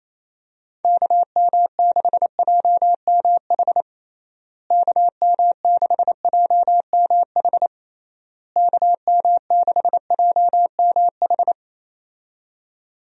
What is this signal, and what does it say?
Morse code, heard as a keyed tone, about 28 wpm, 700 Hz. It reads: KM6JMH KM6JMH KM6JMH
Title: USA Call Signs - 6 Characters - Repeated 3x - 28wpm